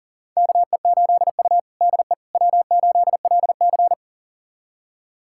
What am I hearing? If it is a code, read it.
Morse code, heard as a keyed tone, about 37 wpm, 700 Hz. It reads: KE8U DE W8LC
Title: USA Callsign DE Callsign - 4 Characters - 40wpm